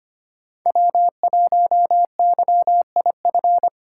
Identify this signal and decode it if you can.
Morse code, heard as a keyed tone, about 25 wpm, 700 Hz. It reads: W1YIF
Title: USA Call Signs - 2x Character Spacing - 25wpm